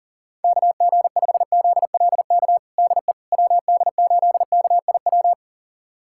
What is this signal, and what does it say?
Morse code, heard as a keyed tone, about 40 wpm, 700 Hz. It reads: KG5ZLK DE WD8KIW
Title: USA Callsign DE Callsign - 6 Characters - 40wpm